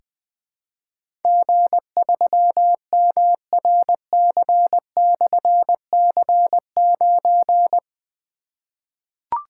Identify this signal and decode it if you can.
Morse code, heard as a keyed tone, about 21 wpm, 700 Hz. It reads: G3MRC/C9
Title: Difficult Call Signs 20wpm